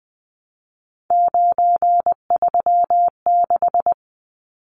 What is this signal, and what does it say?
Morse code, heard as a keyed tone, about 20 wpm, 700 Hz. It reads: 936
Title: Numbers - 3-Digit 20wpm